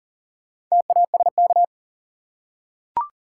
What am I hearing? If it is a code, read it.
Morse code, heard as a keyed tone, about 40 wpm, 700 Hz. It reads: TASK
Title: Four Letter Words - 40wpm